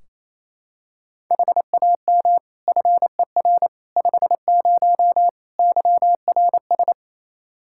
Morse code, heard as a keyed tone, about 28 wpm, 700 Hz. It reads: HAM FER 50 YRS
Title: Practice QSO - Exchange 3 - More Info - 28wpm